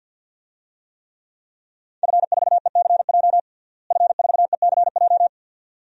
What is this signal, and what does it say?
Morse code, heard as a keyed tone, about 46 wpm, 700 Hz. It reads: F4ECJ F4ECJ
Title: DXCC-100 Call Signs - 5 Characters - Repeated 2x - 50wpm